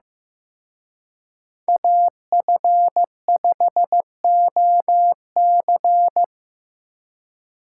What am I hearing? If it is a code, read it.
Morse code, heard as a keyed tone, about 18 wpm, 700 Hz. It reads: AF5OC
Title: Speed Racing - USA 2x2 Call Signs 15wpm